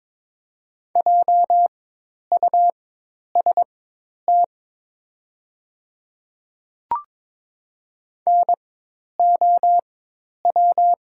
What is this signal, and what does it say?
Morse code, heard as a keyed tone, about 22 wpm, spaced as for 10 wpm, 700 Hz. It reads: JUST NOW
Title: 4x Character Spacing - Top 500 Words - 22wpm